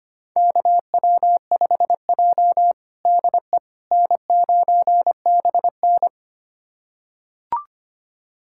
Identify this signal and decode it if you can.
Morse code, heard as a keyed tone, about 25 wpm, 700 Hz. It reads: KW5J DE N9BN
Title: USA Callsign DE Callsign - 4 Characters - 25wpm